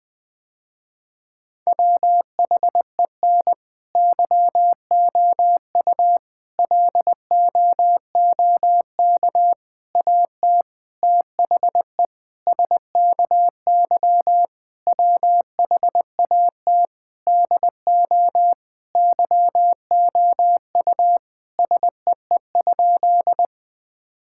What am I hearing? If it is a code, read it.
Morse code, heard as a keyed tone, about 21 wpm, 700 Hz. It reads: WHEN YOU LOOK AT THE SKY WHAT DO YOU SEE?
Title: Sentences with 5-Word Phrases - Spoken Beginning - 20wpm